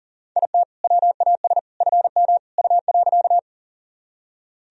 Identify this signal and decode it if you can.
Morse code, heard as a keyed tone, about 40 wpm, 700 Hz. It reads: IT WAS FM U.
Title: Sentences from Top 100 Words and Abbreviations 40wpm